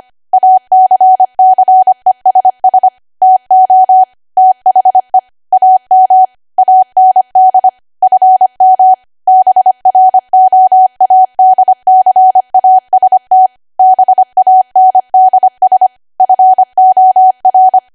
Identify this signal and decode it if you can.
Morse code, heard as a keyed tone, about 25 wpm, 750 Hz. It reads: ACCESS TO THE AM AND FM BROADCAST BANDS FOR
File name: Archive / 25 WPM / 25WPM